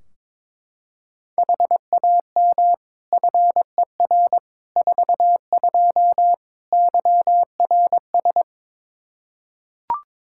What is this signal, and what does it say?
Morse code, heard as a keyed tone, about 22 wpm, 700 Hz. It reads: HAM FER 42 YRS
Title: Practice QSO - Exchange 3 - More Info - 22wpm